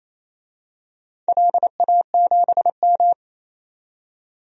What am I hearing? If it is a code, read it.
Morse code, heard as a keyed tone, about 28 wpm, 700 Hz. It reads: LA7M
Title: DXCC-100 Call Signs - 4 Characters - 28wpm